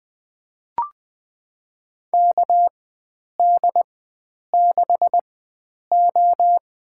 Morse code, heard as a keyed tone, about 20 wpm, spaced as for 9 wpm, 700 Hz. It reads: KD6O
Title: USA Call Signs - 4x Character Spacing - 20wpm